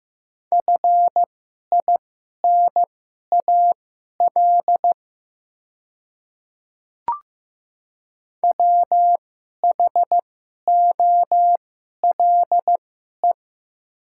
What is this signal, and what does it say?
Morse code, heard as a keyed tone, about 15 wpm, spaced as for 11 wpm, 700 Hz. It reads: FINAL WHOLE
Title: 2x Character Spacing - Top 500 Words - 15wpm